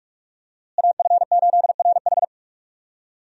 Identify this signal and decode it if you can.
Morse code, heard as a keyed tone, about 46 wpm, 700 Hz. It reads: AF8RH
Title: USA Call Signs - 5 Characters - Repeated 2x - 45wpm